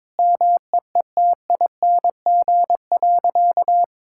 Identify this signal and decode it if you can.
Morse code, heard as a keyed tone, about 22 wpm, 700 Hz. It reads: MEETING.
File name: practice / 22wpm / Sentences with 3-Word Phrases - 22wpm